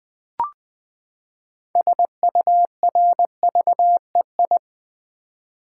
Morse code, heard as a keyed tone, about 20 wpm, 700 Hz. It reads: SURVEI
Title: Build A Word - 6 Letters At A Time - 12-Letter Words - 20wpm